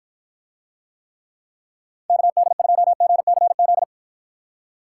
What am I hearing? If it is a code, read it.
Morse code, heard as a keyed tone, about 52 wpm, 700 Hz. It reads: KD2GCZ